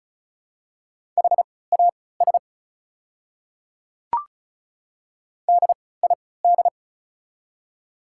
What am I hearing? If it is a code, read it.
Morse code, heard as a keyed tone, about 34 wpm, spaced as for 20 wpm, 690 Hz. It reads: HAS DID
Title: 3x Character Spacing - Top 100 Words - 35wpm